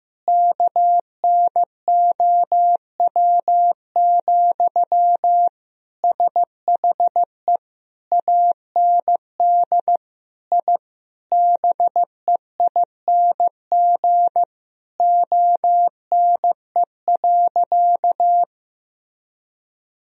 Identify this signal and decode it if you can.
Morse code, heard as a keyed tone, about 16 wpm, 700 Hz. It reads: KNOW, SHE AND I BEING ONE.
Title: Sentences from Top 1000 Words - Review Part 3 - 15wpm